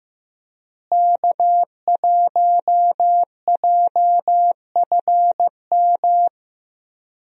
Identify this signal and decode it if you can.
Morse code, heard as a keyed tone, about 15 wpm, 700 Hz. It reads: K1JFM
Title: USA Call Signs - 5 Characters - Repeated 2x - 15wpm